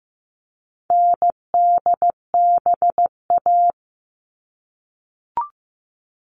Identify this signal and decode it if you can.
Morse code, heard as a keyed tone, about 15 wpm, 700 Hz. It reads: NDBA
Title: All Cut Numbers - 4-Digit 15wpm